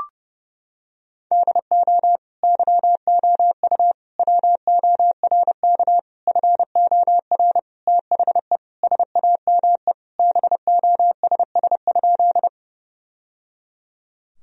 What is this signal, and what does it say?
Morse code, heard as a keyed tone, about 30 wpm, 700 Hz. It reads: DO YOU WORK FOR THE SAME BOSS?
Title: Sentences from Top 1000 Words - Review Part 4 - 30wpm